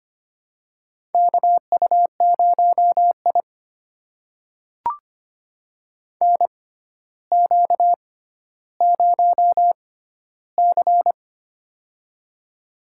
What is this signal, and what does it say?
Morse code, heard as a keyed tone, about 25 wpm, spaced as for 13 wpm, 700 Hz. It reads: KU0I NQ0C
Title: USA Call Signs - 4 Characters - 6x Character Spacing - 25wpm